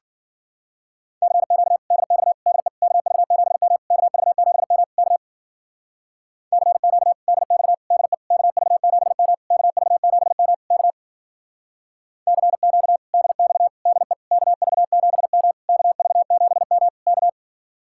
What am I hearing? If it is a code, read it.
Morse code, heard as a keyed tone, about 52 wpm, 700 Hz. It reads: CQ DX DE KV7M KV7M K CQ DX DE KV7M KV7M K CQ DX DE KV7M KV7M K